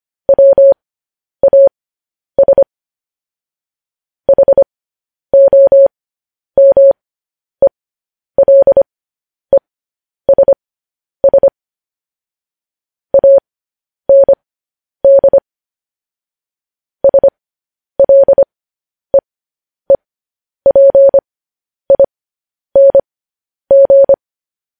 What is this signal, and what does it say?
Morse code, heard as a keyed tone, about 25 wpm, spaced as for 10 wpm, 550 Hz. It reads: WAS HOMELESS AND SLEEPING
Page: news-headlines-in-morse-code-at-10-wpm